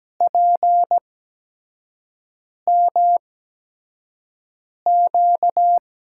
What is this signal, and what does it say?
Morse code, heard as a keyed tone, about 17 wpm, 700 Hz. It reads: P M Q
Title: USA Call Signs - 5 Characters - 8x Character Spacing - 17wpm